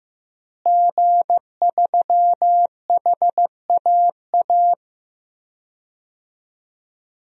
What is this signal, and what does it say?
Morse code, heard as a keyed tone, about 15 wpm, 700 Hz. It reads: G3HAA
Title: DXCC-100 Call Signs - 5 Characters - Repeated 2x - 15wpm